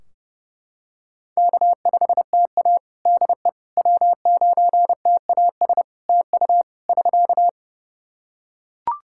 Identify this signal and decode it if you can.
Morse code, heard as a keyed tone, about 29 wpm, 700 Hz. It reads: K5TA DE W9TAS TU <SK>
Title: Practice QSO - Exchange 4 - Ending - 30wpm